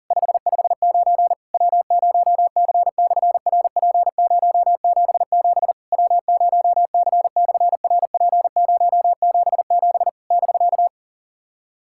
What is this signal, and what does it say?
Morse code, heard as a keyed tone, about 40 wpm, 700 Hz. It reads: 559 W0C/RP077 W0C/RP077 <BK>
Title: SOTA QSO Practice 40wpm